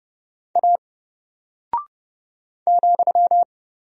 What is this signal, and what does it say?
Morse code, heard as a keyed tone, about 30 wpm, 700 Hz. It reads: A ,
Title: Single Character (Letter:Number:Punctuation:Prosign) Alt 30wpm